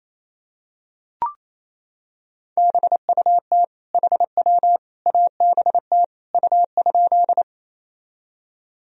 Morse code, heard as a keyed tone, about 28 wpm, 700 Hz. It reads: BUT HW ABT U?
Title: Sentences from Top 100 Words and Abbreviations 28wpm